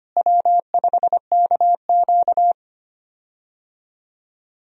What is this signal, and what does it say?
Morse code, heard as a keyed tone, about 25 wpm, 700 Hz. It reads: W5KQ
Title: USA Call Signs - 4 Characters - Repeated 3x - 25wpm